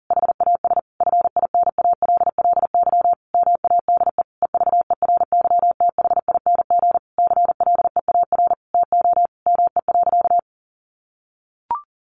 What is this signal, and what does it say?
Morse code, heard as a keyed tone, about 40 wpm, 700 Hz. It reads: HAS FINALLY MADE EVERYTHING CLEAR TO ME.
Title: Sentences from Top 1300 Words 40wpm